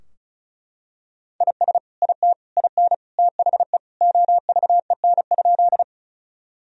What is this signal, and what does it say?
Morse code, heard as a keyed tone, about 36 wpm, 700 Hz. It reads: IS IT IN THE OVEN?
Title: Sentences from Top 1000 Words - Review Part 4 - 35wpm